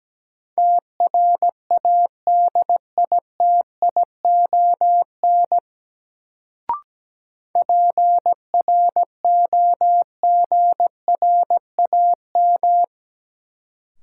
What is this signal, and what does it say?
Morse code, heard as a keyed tone, about 17 wpm, 700 Hz. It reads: TRADITION PROGRAM
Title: Sentences from Top 1500 Words - New Word Drill Alt - 17wpm